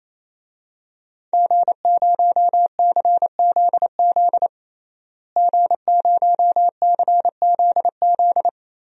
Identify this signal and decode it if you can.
Morse code, heard as a keyed tone, about 28 wpm, 690 Hz. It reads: G0CZZ G0CZZ
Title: DXCC-100 Call Signs - 5 Characters - Repeated 3x - 28wpm